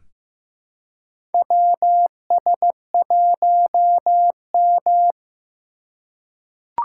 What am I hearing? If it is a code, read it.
Morse code, heard as a keyed tone, about 15 wpm, 700 Hz. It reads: WS1M
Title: USA Call Signs - 2x Character Spacing - 15wpm